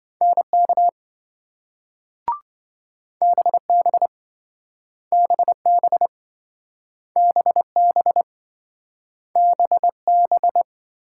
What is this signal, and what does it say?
Morse code, heard as a keyed tone, about 27 wpm, 700 Hz. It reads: NK BB BB BB BB